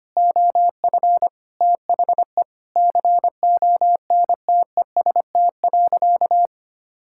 Morse code, heard as a keyed tone, about 25 wpm, 700 Hz. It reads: OF THE CONTEST.
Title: Sentences from Top 1600 Words 25wpm